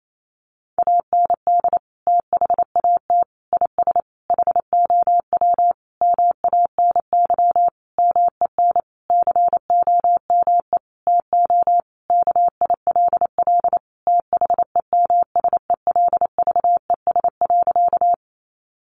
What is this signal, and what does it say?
Morse code, heard as a keyed tone, about 28 wpm, 700 Hz. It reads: AND THAT IS HOW MANY MEN COME TO KILL THEMSELVES.
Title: Sentences from Top 1400 Words 28wpm